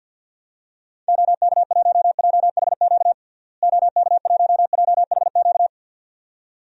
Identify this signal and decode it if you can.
Morse code, heard as a keyed tone, about 51 wpm, 700 Hz. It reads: OK1JHQ OK1JHQ
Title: DXCC-100 Call Signs - 6 Characters - Repeated 2x - 50wpm